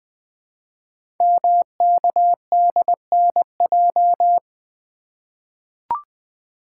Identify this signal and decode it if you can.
Morse code, heard as a keyed tone, about 20 wpm, 700 Hz. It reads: MKDNJ
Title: Code Groups - 5 Letters - 20wpm